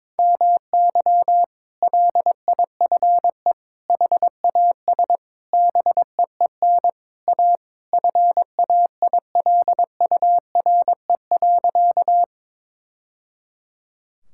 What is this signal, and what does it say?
Morse code, heard as a keyed tone, about 22 wpm, 700 Hz. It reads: MY LIFE HAS BEEN A FAILURE.
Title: Sentences from Top 1000 Words - Review Part 6 - 22wpm